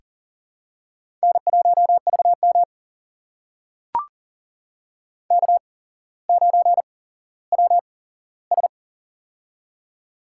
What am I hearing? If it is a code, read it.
Morse code, heard as a keyed tone, about 40 wpm, spaced as for 17 wpm, 700 Hz. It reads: N1VM K9WS
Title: USA Call Signs - 4 Characters - 8x Character Spacing - 40wpm